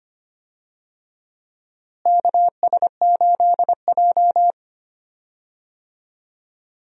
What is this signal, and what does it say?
Morse code, heard as a keyed tone, about 25 wpm, 700 Hz. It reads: KS8J
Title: USA Call Signs - 4 Characters - Repeated 3x - 25wpm